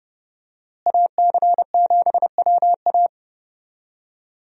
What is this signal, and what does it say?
Morse code, heard as a keyed tone, about 30 wpm, 700 Hz. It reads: AC7WA
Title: USA Call Signs - 5 Characters - Repeated 2x - 30wpm